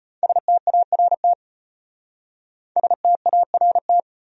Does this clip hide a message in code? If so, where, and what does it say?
Yes: Morse code from 0.2 to 4.0 s, heard as a keyed tone, about 33 wpm, 700 Hz. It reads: START START